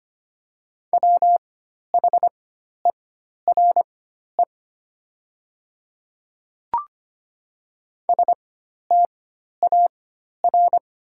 Morse code, heard as a keyed tone, about 25 wpm, spaced as for 12 wpm, 700 Hz. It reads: WHERE STAR
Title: 4x Character Spacing - Top 500 Words - 25wpm